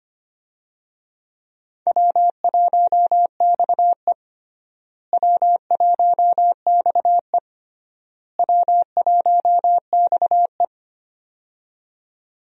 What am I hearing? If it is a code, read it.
Morse code, heard as a keyed tone, about 25 wpm, 700 Hz. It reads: W1XE W1XE W1XE
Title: USA Call Signs - 4 Characters - Repeated 3x - 25wpm